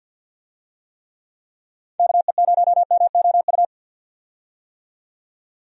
Morse code, heard as a keyed tone, about 50 wpm, 700 Hz. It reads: KE0MOU
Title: USA Call Signs - 6 Characters - 50wpm